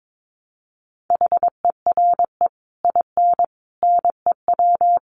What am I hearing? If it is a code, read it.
Morse code, heard as a keyed tone, about 22 wpm, 700 Hz. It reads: HERE IN NEW
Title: Sets of 5 Words Encore2 22wpm